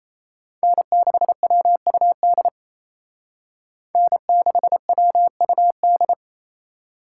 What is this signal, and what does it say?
Morse code, heard as a keyed tone, about 34 wpm, 700 Hz. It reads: N6WUD N6WUD
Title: Speed Racing - USA 1x3 Call Signs 28wpm